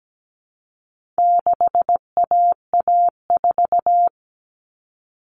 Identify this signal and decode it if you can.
Morse code, heard as a keyed tone, about 17 wpm, 700 Hz. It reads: 6AA4
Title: All Cut Numbers - 4-Digit 17wpm